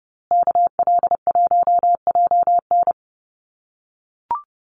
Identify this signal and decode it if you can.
Morse code, heard as a keyed tone, about 29 wpm, 700 Hz. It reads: KL1JN
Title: USA Call Signs Repeated 3x 30wpm